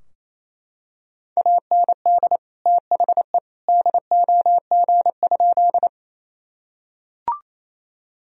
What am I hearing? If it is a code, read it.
Morse code, heard as a keyed tone, about 28 wpm, 700 Hz. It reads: AND THE DOG?
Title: Sentences from Top 1000 Words - Review Part 2 - 28wpm